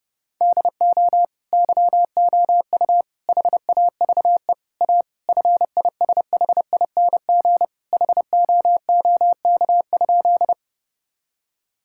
Morse code, heard as a keyed tone, about 30 wpm, 700 Hz. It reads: DO YOU HAVE A FISHING HOOK?
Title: Sentences from Top 1600 Words 30wpm